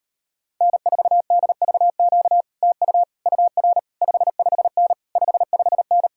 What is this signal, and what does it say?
Morse code, heard as a keyed tone, about 33 wpm, 700 Hz. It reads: N4DVQ TU UR 55N 55N